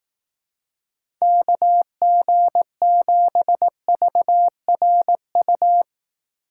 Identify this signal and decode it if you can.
Morse code, heard as a keyed tone, about 18 wpm, 700 Hz. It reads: KG7VRU
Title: Speed Racing - USA 2x3 Call Signs 15wpm